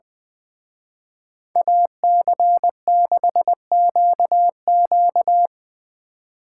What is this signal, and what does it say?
Morse code, heard as a keyed tone, about 23 wpm, 700 Hz. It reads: AC6QQ
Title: Speed Racing - USA Call Signs 20wpm